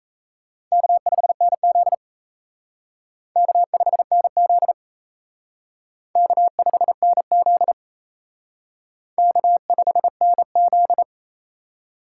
Morse code, heard as a keyed tone, about 36 wpm, 700 Hz. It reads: K5NZ K5NZ K5NZ K5NZ